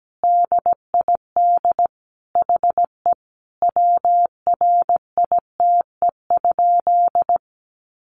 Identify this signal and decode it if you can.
Morse code, heard as a keyed tone, about 17 wpm, 700 Hz. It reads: DID HE WRITE?
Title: Sentences from Top 800 Words 17wpm